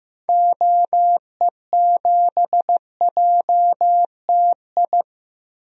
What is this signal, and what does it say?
Morse code, heard as a keyed tone, about 15 wpm, 700 Hz. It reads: OE7JTI
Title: DXCC-100 Call Signs - 6 Characters - Repeated 3x - 15wpm